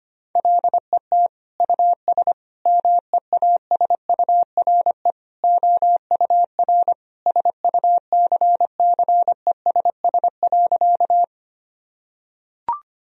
Morse code, heard as a keyed tone, about 25 wpm, 700 Hz. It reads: LET US MEASURE OUR SUCCESS.
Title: Sentences from Top 1000 Words - Review Part 2 - 25wpm